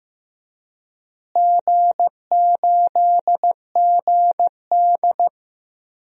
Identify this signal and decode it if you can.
Morse code, heard as a keyed tone, about 15 wpm, 700 Hz. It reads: G8GD
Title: DXCC-100 Call Signs - 4 Characters - Repeated 3x - 15wpm